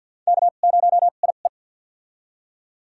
Morse code, heard as a keyed tone, about 53 wpm, spaced as for 37 wpm, 700 Hz. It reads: K0IE
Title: USA Call Signs - 2x Character Spacing - 50wpm